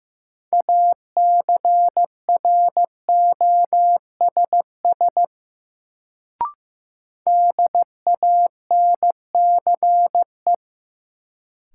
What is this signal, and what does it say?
Morse code, heard as a keyed tone, about 15 wpm, 700 Hz. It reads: ACROSS DANCE
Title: Sentences from Top 700 Words - New Word Drill - 15wpm